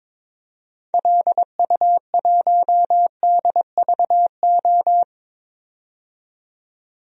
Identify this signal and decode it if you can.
Morse code, heard as a keyed tone, about 22 wpm, 700 Hz. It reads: LU1DVO
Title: DXCC-100 Call Signs - 6 Characters - 22wpm